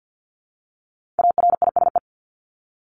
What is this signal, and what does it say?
Morse code, heard as a keyed tone, about 48 wpm, 700 Hz. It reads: ARISE
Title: Top 1501-1750 Words 50wpm